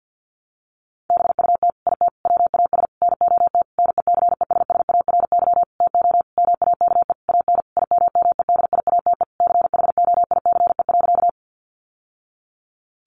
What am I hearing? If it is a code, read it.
Morse code, heard as a keyed tone, about 49 wpm, 700 Hz. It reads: BUT IT WAS NOT NECESSARY TO MAKE AN IMMEDIATE CHOICE.
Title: Sentences from Top 1200 Words 50wpm